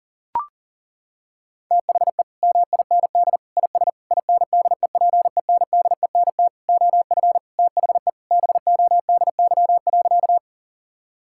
Morse code, heard as a keyed tone, about 40 wpm, 700 Hz. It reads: THE MIND IS INDEPENDENT OF THE BODY.
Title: Sentences from Top 1000 Words - Review Part 6 - 40wpm